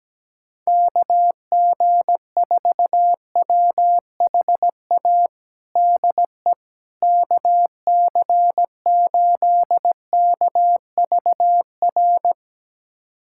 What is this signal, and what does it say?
Morse code, heard as a keyed tone, about 17 wpm, 700 Hz. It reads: KG4WHA DE KC8KVR
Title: USA Callsign DE Callsign - 6 Characters - 17wpm